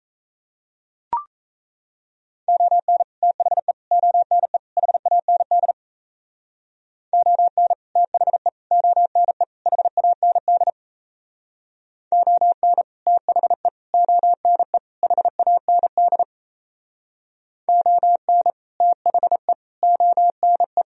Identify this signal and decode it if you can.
Morse code, heard as a keyed tone, about 37 wpm, 700 Hz. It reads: ON THE ONE HAND ON THE ONE HAND ON THE ONE HAND ON THE ONE